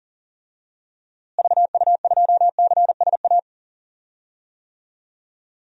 Morse code, heard as a keyed tone, about 37 wpm, 700 Hz. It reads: VU2CSA
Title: DXCC-100 Call Signs - 6 Characters - Repeated 2x - 40wpm